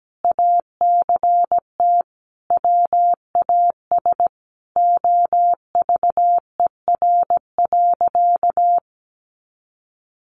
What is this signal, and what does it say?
Morse code, heard as a keyed tone, about 17 wpm, 700 Hz. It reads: ACT WAS OVER.
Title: Sentences from Top 200 Words 17wpm